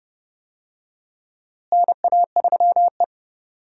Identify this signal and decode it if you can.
Morse code, heard as a keyed tone, about 32 wpm, 700 Hz. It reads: NA3E
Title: USA Call Signs - 4 Characters - Repeated 3x - 30wpm